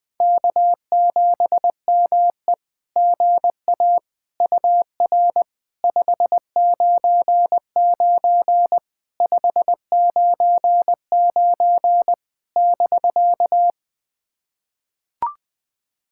Morse code, heard as a keyed tone, about 20 wpm, 700 Hz. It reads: K7ME GA UR 599 599 <BK>
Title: SOTA QSO Practice 20wpm